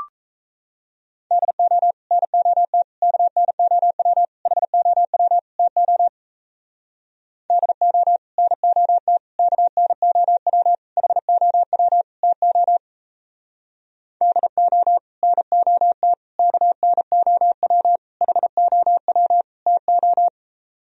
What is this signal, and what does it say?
Morse code, heard as a keyed tone, about 38 wpm, 700 Hz. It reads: DO NOT KNOW HOW TO DO NOT KNOW HOW TO DO NOT KNOW HOW TO